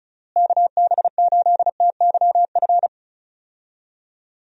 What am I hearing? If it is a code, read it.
Morse code, heard as a keyed tone, about 36 wpm, 700 Hz. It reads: KB8TYF
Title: USA Call Signs - 6 Characters - Repeated 2x - 35wpm